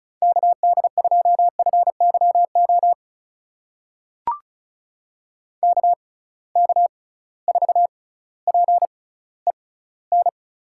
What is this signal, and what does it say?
Morse code, heard as a keyed tone, about 36 wpm, spaced as for 18 wpm, 700 Hz. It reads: KD2FYO KK4PEN
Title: USA Call Signs - 6 Characters - 6x Character Spacing - 35wpm